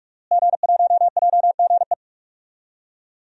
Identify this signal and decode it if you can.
Morse code, heard as a keyed tone, about 47 wpm, 700 Hz. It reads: G1JGE
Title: DXCC-100 Call Signs - 5 Characters - Repeated 3x - 45wpm